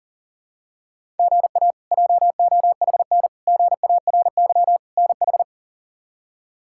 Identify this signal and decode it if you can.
Morse code, heard as a keyed tone, about 37 wpm, 700 Hz. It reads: GA JOHN GARY NH
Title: K1USN SST QSO Practice - 40wpm